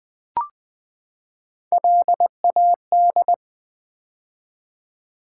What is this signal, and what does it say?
Morse code, heard as a keyed tone, about 20 wpm, 700 Hz. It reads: LAD
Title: Build A Word - 3 Letters At A Time - 6-Letter Words - 20wpm